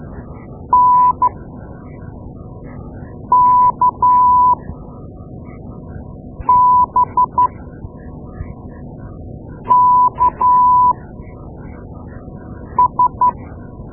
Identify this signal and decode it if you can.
Morse code, heard as a keyed tone, about 12 wpm, spaced as for 4 wpm, 990 Hz. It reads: NKBKS
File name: Morse Lessons / Morse Lessons / Tape-2A